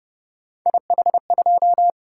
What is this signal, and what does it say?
Morse code, heard as a keyed tone, about 28 wpm, 700 Hz. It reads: IH2